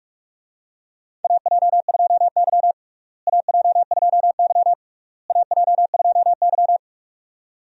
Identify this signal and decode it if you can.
Morse code, heard as a keyed tone, about 41 wpm, 700 Hz. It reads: AJ2Y AJ2Y AJ2Y
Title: USA Call Signs - 4 Characters - Repeated 3x - 45wpm